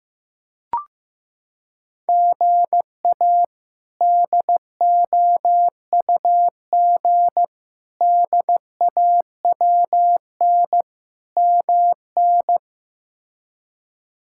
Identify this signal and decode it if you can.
Morse code, heard as a keyed tone, about 15 wpm, 700 Hz. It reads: GA DOUG DAWN MN
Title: K1USN SST QSO Practice - 15wpm